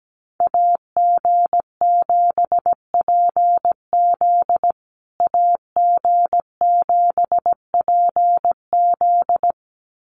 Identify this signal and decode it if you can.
Morse code, heard as a keyed tone, about 17 wpm, 700 Hz. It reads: AG7PZ AG7PZ
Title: USA Call Signs Repeated 2x 17wpm